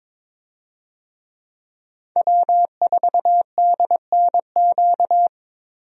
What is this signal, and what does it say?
Morse code, heard as a keyed tone, about 22 wpm, 700 Hz. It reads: W4DNQ
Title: USA Call Signs - 5 Characters - Repeated 3x - 22wpm